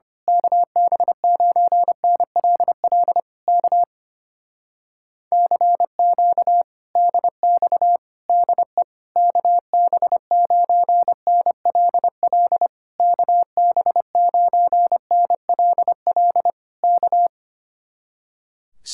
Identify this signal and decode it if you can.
Morse code, heard as a keyed tone, about 24 wpm, 700 Hz. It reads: KB9NLL K CQ DX DE KB9NLL KB9NLL K
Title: Speed-Racing Common QSO Phrases Encore - 25wpm